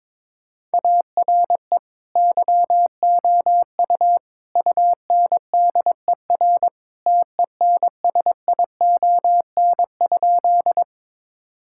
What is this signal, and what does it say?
Morse code, heard as a keyed tone, about 21 wpm, 700 Hz. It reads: ARE YOU UNDER TENSION?
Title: Sentences from Top 1600 Words 22wpm